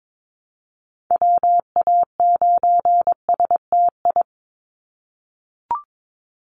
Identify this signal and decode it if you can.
Morse code, heard as a keyed tone, about 22 wpm, 700 Hz. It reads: WA9STI
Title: NA SOTA Activator Callsigns 22wpm